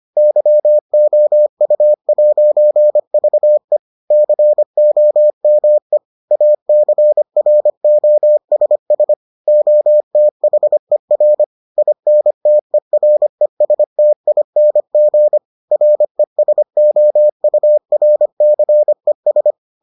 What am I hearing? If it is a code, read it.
Morse code, heard as a keyed tone, about 25 wpm, 600 Hz. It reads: YOU'VE COME ACROSS OTHER INTERESTING RESOURCES